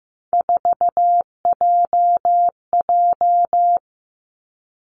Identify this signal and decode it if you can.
Morse code, heard as a keyed tone, about 15 wpm, 700 Hz. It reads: 4JJ
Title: Three Challenging Letter-Number Combinations 15wpm